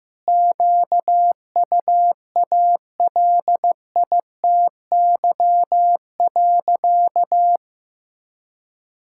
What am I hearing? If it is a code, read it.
Morse code, heard as a keyed tone, about 15 wpm, 700 Hz. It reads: QUALITY.
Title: Sentences from Top 1000 Words - Review Part 5 - 15wpm